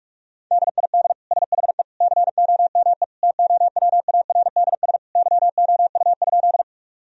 Morse code, heard as a keyed tone, about 41 wpm, 700 Hz. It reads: DID SHE COME TOWARDS YOU?
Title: Sentences from Top 1000 Words - Review Part 2 - 45wpm